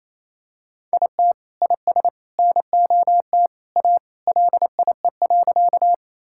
Morse code, heard as a keyed tone, about 27 wpm, 700 Hz. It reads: IT IS NOT A LIE.
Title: Sentences from Top 1600 Words 28wpm